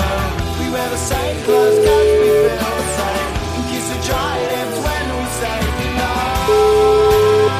Ringback tone (Europe, 425 Hz), with noise.